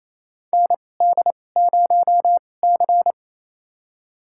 Morse code, heard as a keyed tone, about 28 wpm, spaced as for 20 wpm, 700 Hz. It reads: ND0C
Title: USA Call Signs - 4 Characters - 2x Character Spacing - 28wpm